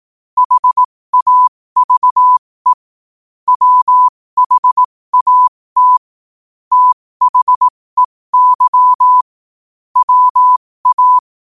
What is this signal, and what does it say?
Morse code, heard as a keyed tone, about 18 wpm, spaced as for 15 wpm, 1000 Hz. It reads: HAVE WHAT THEY WA
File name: qotdfiles / 15wpm